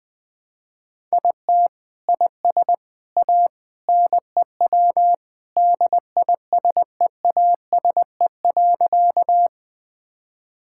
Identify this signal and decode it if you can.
Morse code, heard as a keyed tone, about 20 wpm, 700 Hz. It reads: IT IS A NEW DISEASE.
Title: Sentences from Top 1000 Words - Review Part 6 - 20wpm